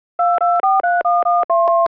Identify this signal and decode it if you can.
DTMF, heard as telephone touch tones, tones of about 300 ms. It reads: 2431